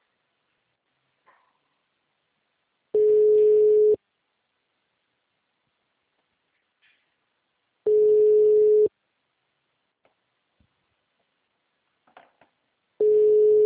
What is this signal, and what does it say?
Ringback tone (Europe, 425 Hz).